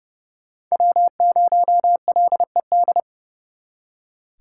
Morse code, heard as a keyed tone, about 39 wpm, 700 Hz. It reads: W0LED